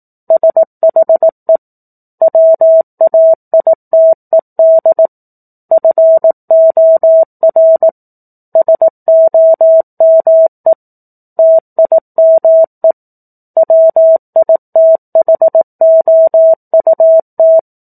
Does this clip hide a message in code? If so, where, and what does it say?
Yes: Morse code from 0.3 to 17.6 s, heard as a keyed tone, about 18 wpm, 650 Hz. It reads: SHE WAITED FOR SOME TIME WITHOUT